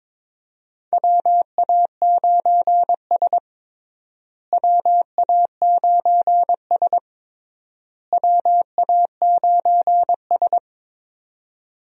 Morse code, heard as a keyed tone, about 22 wpm, 700 Hz. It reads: WA9S WA9S WA9S
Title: USA Call Signs - 4 Characters - Repeated 3x - 22wpm